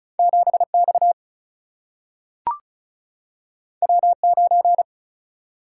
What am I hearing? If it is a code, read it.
Morse code, heard as a keyed tone, about 36 wpm, 700 Hz. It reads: 7X W9
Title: Single-Letter Single-Number 35wpm